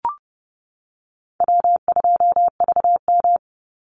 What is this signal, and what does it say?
Morse code, heard as a keyed tone, about 30 wpm, 700 Hz. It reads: W2VM
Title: Call Signs - 30wpm